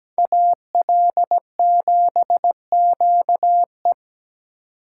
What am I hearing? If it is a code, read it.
Morse code, heard as a keyed tone, about 20 wpm, 700 Hz. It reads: AL7QE
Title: Speed Racing - USA 2x2 Call Signs 17wpm